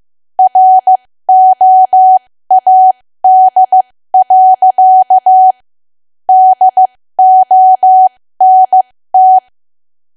Morse code, heard as a keyed tone, about 15 wpm, spaced as for 13 wpm, 750 Hz. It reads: ROAD. DONT
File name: cwa / W1AW-Level2 / 13WPM